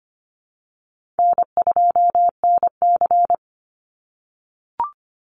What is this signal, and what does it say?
Morse code, heard as a keyed tone, about 25 wpm, 700 Hz. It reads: N2NC
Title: Easy Call Signs - 25wpm